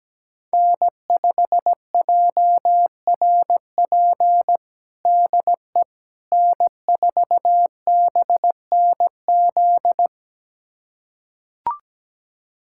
Morse code, heard as a keyed tone, about 17 wpm, 700 Hz. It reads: N5JRP DE N4BNZ
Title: USA Callsign DE Callsign - 5 Characters - 17wpm